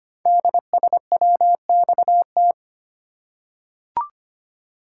Morse code, heard as a keyed tone, about 25 wpm, 700 Hz. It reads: DSWXT
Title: Code Groups - 5 Letters - 25wpm